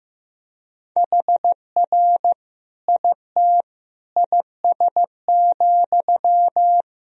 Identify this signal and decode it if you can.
Morse code, heard as a keyed tone, about 15 wpm, 700 Hz. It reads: HR IT IS,
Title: Sentences from Top 200 Words and Abbreviations 15wpm